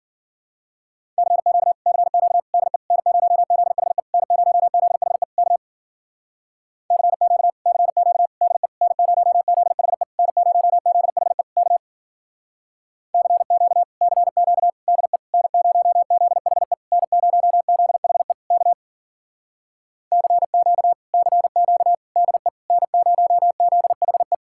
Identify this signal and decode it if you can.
Morse code, heard as a keyed tone, about 50 wpm, 700 Hz. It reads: CQ CQ DE N0ZHE N0ZHE K CQ CQ DE N0ZHE N0ZHE K CQ CQ DE N0ZHE N0ZHE K CQ CQ DE N0ZHE
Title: Speed-Racing Common QSO Phrases Encore - 40wpm